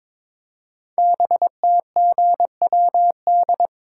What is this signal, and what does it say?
Morse code, heard as a keyed tone, about 22 wpm, 700 Hz. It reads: BTGWD
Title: Code Groups - 5 Letters - 22wpm